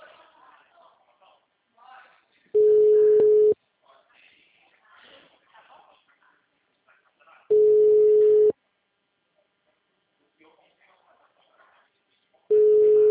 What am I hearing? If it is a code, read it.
Ringback tone (Europe, 425 Hz), with speech.
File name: out- -2014- - - .13588